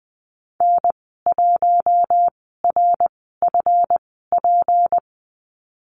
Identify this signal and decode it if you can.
Morse code, heard as a keyed tone, about 20 wpm, spaced as for 15 wpm, 700 Hz. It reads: N1RFP
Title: 186 Lesson 30g - Letter b - Call Signs 20wpm